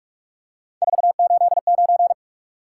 Morse code, heard as a keyed tone, about 44 wpm, 700 Hz. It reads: 489
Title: Top 180 QSO Elements 45wpm